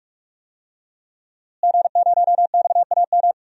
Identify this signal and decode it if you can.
Morse code, heard as a keyed tone, about 42 wpm, 700 Hz. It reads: G0XAM
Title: DXCC-100 Call Signs - 5 Characters - Repeated 3x - 45wpm